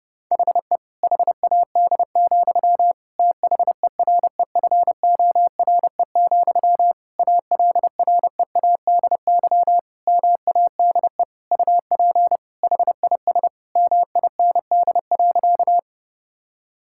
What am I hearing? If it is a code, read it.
Morse code, heard as a keyed tone, about 30 wpm, 700 Hz. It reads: HE HAD, THEREFORE, ALREADY MADE UP HIS MIND.
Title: Sentences from Top 1000 Words - Review Part 5 - 30wpm